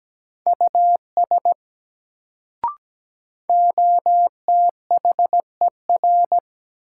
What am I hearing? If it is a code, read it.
Morse code, heard as a keyed tone, about 17 wpm, 700 Hz. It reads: US OTHER
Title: Sets of 2 Words - New Word Drill - 17wpm